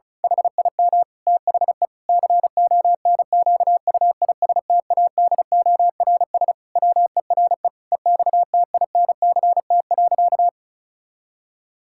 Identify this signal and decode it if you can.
Morse code, heard as a keyed tone, about 36 wpm, 700 Hz. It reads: HIM THE CONQUISTADORS WERE EXTINCT.
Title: Pangram Challenge v2 - 35wpm